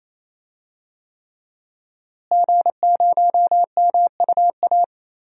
Morse code, heard as a keyed tone, about 28 wpm, 700 Hz. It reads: G0MUA
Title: DXCC-100 Call Signs - 5 Characters - Repeated 3x - 28wpm